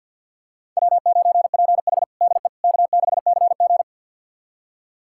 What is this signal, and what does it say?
Morse code, heard as a keyed tone, about 49 wpm, 700 Hz. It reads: W9PH DE K6CG
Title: USA Callsign DE Callsign - 4 Characters - 50wpm